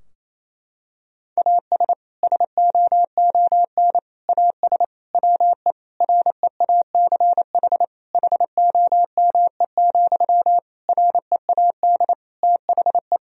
Morse code, heard as a keyed tone, about 28 wpm, 700 Hz. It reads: AS SOON AS WE REACH HOME, READ THE
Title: Sentences from Top 1000 Words - Review Part 2 - 28wpm